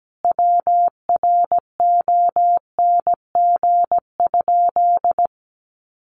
Morse code, heard as a keyed tone, about 17 wpm, 700 Hz. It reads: WRONG?
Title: Sentences from Top 1100 Words 17wpm